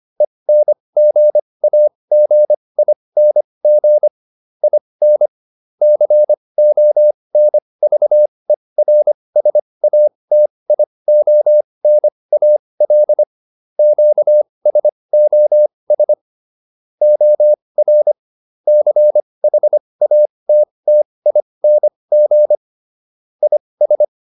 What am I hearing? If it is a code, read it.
Morse code, heard as a keyed tone, about 25 wpm, spaced as for 20 wpm, 600 Hz. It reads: ENGAGING IN CONVERSATIONAL QSOS OR CHATTING IS
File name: SS206 20